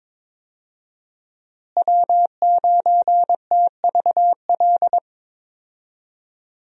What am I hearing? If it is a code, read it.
Morse code, heard as a keyed tone, about 22 wpm, 700 Hz. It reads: W9TVL
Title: USA Call Signs - 5 Characters - Repeated 2x - 22wpm